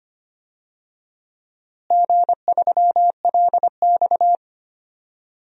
Morse code, heard as a keyed tone, about 25 wpm, 700 Hz. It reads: G3LX
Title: DXCC-100 Call Signs - 4 Characters - Repeated 2x - 25wpm